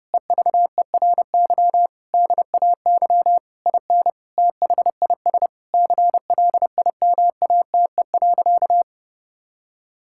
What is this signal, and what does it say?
Morse code, heard as a keyed tone, about 30 wpm, 700 Hz. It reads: EVERY DAY IN THIS CLIMATE.
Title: Sentences from Top 1600 Words 30wpm